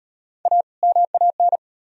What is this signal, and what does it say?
Morse code, heard as a keyed tone, about 38 wpm, 700 Hz. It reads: A MAN